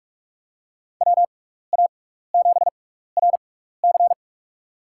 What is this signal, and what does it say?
Morse code, heard as a keyed tone, about 44 wpm, spaced as for 15 wpm, 700 Hz. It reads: WA7RC
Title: USA Call Signs - 5 Characters - 6x Character Spacing - 45wpm